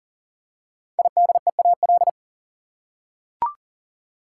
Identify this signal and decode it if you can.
Morse code, heard as a keyed tone, about 40 wpm, 700 Hz. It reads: IDEAL
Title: Five Letter Words - 40wpm